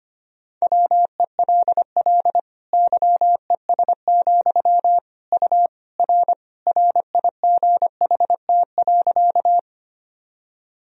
Morse code, heard as a keyed tone, about 25 wpm, 700 Hz. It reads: WELL YES, U R RIGHT.
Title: Sentences from Top 500 Words and Abbreviations 25wpm